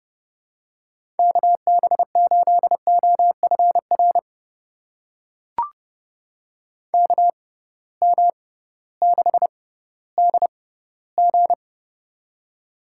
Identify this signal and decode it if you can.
Morse code, heard as a keyed tone, about 30 wpm, spaced as for 16 wpm, 700 Hz. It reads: KB8OFR KM6DG
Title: USA Call Signs - 6x Character Spacing - 30wpm